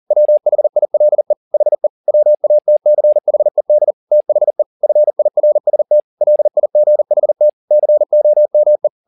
Morse code, heard as a keyed tone, about 40 wpm, 600 Hz. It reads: WHILE HE WATCHED THE FIRST LIGHT COME
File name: the old man and the sea 2 40wpm